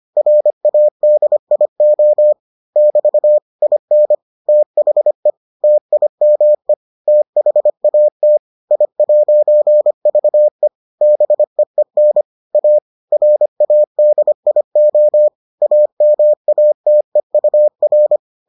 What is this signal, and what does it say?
Morse code, heard as a keyed tone, about 25 wpm, 600 Hz. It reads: RADIO <BT> IN THE TIME THAT I'VE BEEN A RADIO AMATEUR